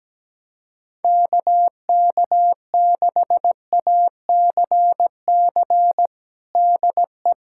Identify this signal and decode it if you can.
Morse code, heard as a keyed tone, about 17 wpm, 700 Hz. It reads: KK6ACC DE
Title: USA Callsign DE Callsign - 6 Characters - 17wpm